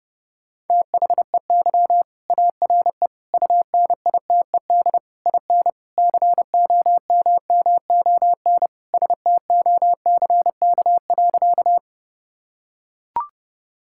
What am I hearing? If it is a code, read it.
Morse code, heard as a keyed tone, about 32 wpm, 700 Hz. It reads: THEY ARE UNITED IN COMMON STOCK.
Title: Sentences from Top 1000 Words - Review Part 5 - 30wpm